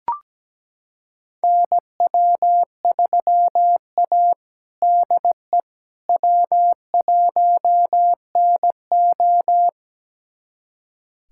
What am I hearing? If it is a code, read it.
Morse code, heard as a keyed tone, about 17 wpm, 700 Hz. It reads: NW3A DE W1NO
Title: USA Callsign DE Callsign - 4 Characters - 17wpm